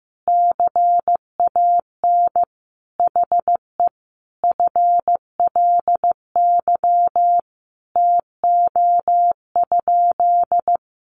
Morse code, heard as a keyed tone, about 15 wpm, 700 Hz. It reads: CAN HE FLY TO?
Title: Sentences from Top 500 Words 15wpm